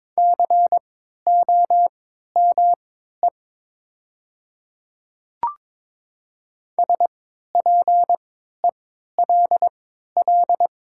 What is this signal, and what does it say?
Morse code, heard as a keyed tone, about 22 wpm, spaced as for 12 wpm, 700 Hz. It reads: COME SPELL
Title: 3x Character Spacing - Top 300 Words - 22wpm